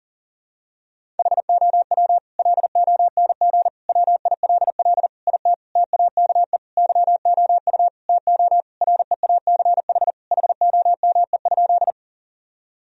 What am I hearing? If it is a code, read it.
Morse code, heard as a keyed tone, about 40 wpm, 700 Hz. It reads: HOW LONG WILL IT TAKE YOU TO REACH HOME?
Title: Sentences from Top 1000 Words - Review Part 2 - 40wpm